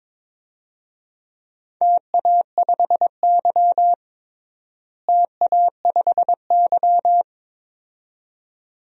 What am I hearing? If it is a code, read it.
Morse code, heard as a keyed tone, about 22 wpm, 700 Hz. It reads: TA5Y TA5Y
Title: DXCC-100 Call Signs - 4 Characters - Repeated 2x - 22wpm